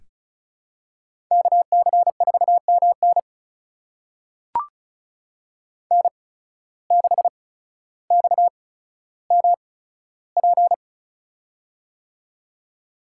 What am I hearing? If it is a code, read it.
Morse code, heard as a keyed tone, about 36 wpm, spaced as for 15 wpm, 700 Hz. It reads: KC4MN N6XMP
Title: USA Call Signs - 8x Character Spacing - 35wpm